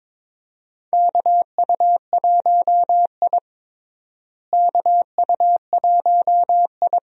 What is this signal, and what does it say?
Morse code, heard as a keyed tone, about 22 wpm, 700 Hz. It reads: KU1I KU1I
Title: USA Call Signs - 4 Characters - Repeated 2x - 22wpm